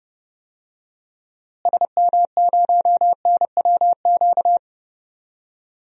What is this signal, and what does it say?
Morse code, heard as a keyed tone, about 30 wpm, 700 Hz. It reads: SM0NWQ
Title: DXCC-100 Call Signs - 6 Characters - Repeated 3x - 30wpm